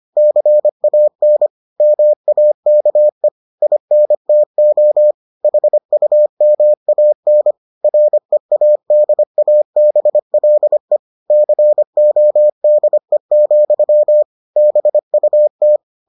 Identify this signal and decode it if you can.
Morse code, heard as a keyed tone, about 25 wpm, 600 Hz. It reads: CAN MAKE INTO HUMAN READABLE CODE, BUT